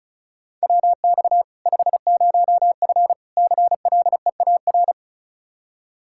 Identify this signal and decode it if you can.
Morse code, heard as a keyed tone, about 36 wpm, 700 Hz. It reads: WX 50F CLEAR
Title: Practice QSO - Exchange 2 - Info - 35wpm